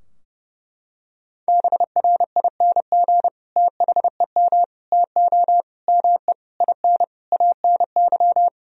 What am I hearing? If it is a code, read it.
Morse code, heard as a keyed tone, about 32 wpm, 700 Hz. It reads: BRING THEM TO ME IN ANY
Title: Sentences from Top 1000 Words - Review Part 3 - 30wpm